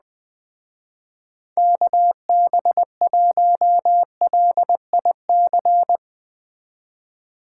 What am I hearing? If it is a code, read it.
Morse code, heard as a keyed tone, about 20 wpm, 700 Hz. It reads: KB1LIC